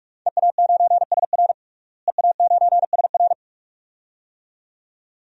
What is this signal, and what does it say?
Morse code, heard as a keyed tone, about 41 wpm, 700 Hz. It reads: EA9SR EA9SR
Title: DXCC-100 Call Signs - 5 Characters - Repeated 2x - 45wpm